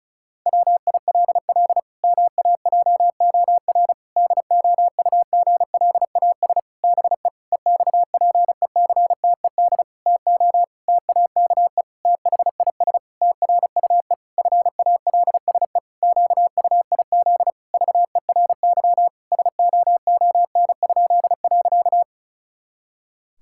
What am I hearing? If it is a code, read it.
Morse code, heard as a keyed tone, about 34 wpm, 700 Hz. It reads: WILL MAJOR DOUGLAS BE EXPECTED TO TAKE THIS TRUE FALSE QUIZ VERY SOON?.
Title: Pangram Challenge v2 - 35wpm